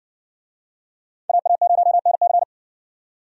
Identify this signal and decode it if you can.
Morse code, heard as a keyed tone, about 60 wpm, 700 Hz. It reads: AA0NC